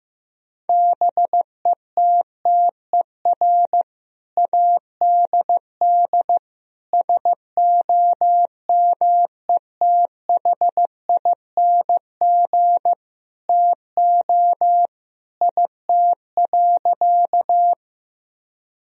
Morse code, heard as a keyed tone, about 15 wpm, 700 Hz. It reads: BETTER ADD SOMETHING TO IT.
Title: Sentences from Top 1000 Words - Review Part 1 - 15wpm